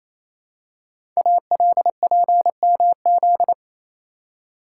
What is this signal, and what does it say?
Morse code, heard as a keyed tone, about 28 wpm, 700 Hz. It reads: ALPMZ
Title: Code Groups - 5 Letter:Numbers - 28wpm